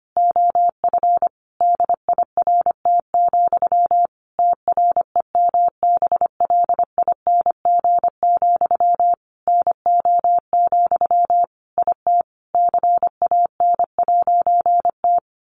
Morse code, heard as a keyed tone, about 25 wpm, 700 Hz. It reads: OF DIRT, TREMBLING, NO, IT CAN'T
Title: A Sound Of Thunder 25wpm